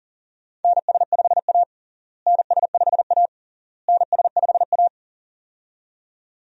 Morse code, heard as a keyed tone, about 40 wpm, 700 Hz. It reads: NS5A NS5A NS5A
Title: USA Call Signs - 4 Characters - Repeated 3x - 40wpm